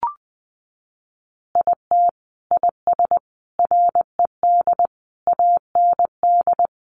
Morse code, heard as a keyed tone, about 20 wpm, 700 Hz. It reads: IT IS RED AND
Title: Additional Easy Phrases 20wpm